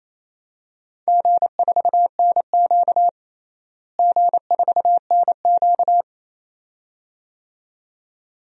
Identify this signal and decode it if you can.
Morse code, heard as a keyed tone, about 28 wpm, 700 Hz. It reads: G4NQ G4NQ
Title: DXCC-100 Call Signs - 4 Characters - Repeated 2x - 28wpm